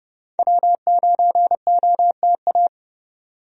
Morse code, heard as a keyed tone, about 32 wpm, 700 Hz. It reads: W9OTA
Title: USA Call Signs - 5 Characters - 30wpm